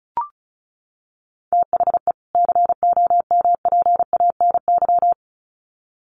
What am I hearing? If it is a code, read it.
Morse code, heard as a keyed tone, about 34 wpm, 700 Hz. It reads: THE COMPANY
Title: Sets of 2 Words 35wpm